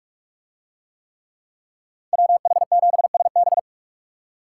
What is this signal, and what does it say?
Morse code, heard as a keyed tone, about 44 wpm, 700 Hz. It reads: WH7SB